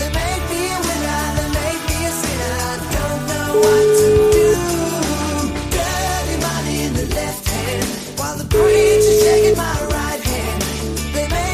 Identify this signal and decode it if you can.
Ringback tone (Europe, 425 Hz), with noise.